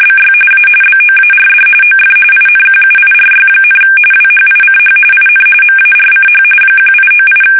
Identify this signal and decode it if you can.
RTTY, about 45.45 baud. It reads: BY A BOARD OF INQUIRY, WHICH IN MARCH 1898
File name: fsk850 / usnavy3